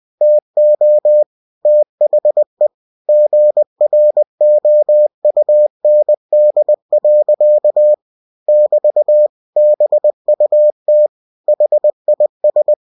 Morse code, heard as a keyed tone, about 20 wpm, 600 Hz. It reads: TO THE GROUND. <BT> BUT HIS